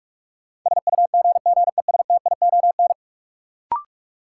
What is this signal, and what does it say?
Morse code, heard as a keyed tone, about 42 wpm, 700 Hz. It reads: SUGGESTION
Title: Sentences from Top 1600 Words - New Word Drill Alt - 45wpm